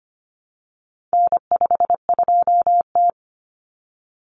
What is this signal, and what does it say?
Morse code, heard as a keyed tone, about 25 wpm, 700 Hz. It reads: N52T
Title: Common Cut Numbers - 4-Digit 25wpm